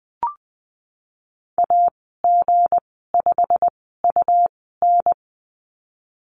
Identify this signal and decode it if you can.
Morse code, heard as a keyed tone, about 20 wpm, spaced as for 15 wpm, 700 Hz. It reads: AG5UN
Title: 173 Lesson 28g - Number 9 - Call Signs 20wpm